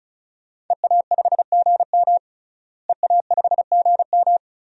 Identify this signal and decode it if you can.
Morse code, heard as a keyed tone, about 36 wpm, 700 Hz. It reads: EA5GM EA5GM
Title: DXCC-100 Call Signs - 5 Characters - Repeated 2x - 35wpm